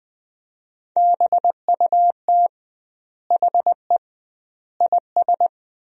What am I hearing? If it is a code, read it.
Morse code, heard as a keyed tone, about 20 wpm, 700 Hz. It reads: BUT HE IS
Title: 2x Word-Spacing - Sentences from Top 200 Words - 20wpm